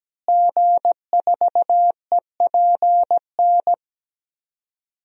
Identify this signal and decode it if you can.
Morse code, heard as a keyed tone, about 17 wpm, 700 Hz. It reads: G4EPN
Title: DXCC-100 Call Signs - 5 Characters - Repeated 2x - 17wpm